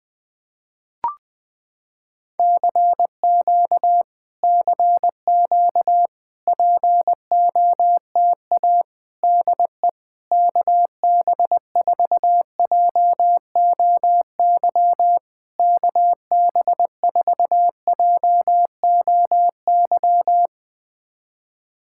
Morse code, heard as a keyed tone, about 20 wpm, 690 Hz. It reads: CQ CQ POTA DE KB4JOY KB4JOY
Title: POTA QSO Practice - 20wpm